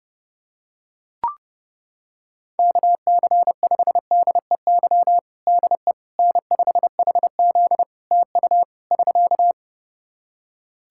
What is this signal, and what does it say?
Morse code, heard as a keyed tone, about 30 wpm, 700 Hz. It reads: KC5DEY DE N5HZ TU <SK>
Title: Practice QSO - Exchange 4 - Ending - 30wpm